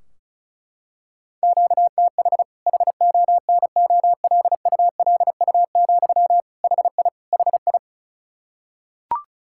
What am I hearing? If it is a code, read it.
Morse code, heard as a keyed tone, about 36 wpm, 700 Hz. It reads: QTH HONOLULU, HI HI
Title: Practice QSO - Exchange 1 - Beginning - 35wpm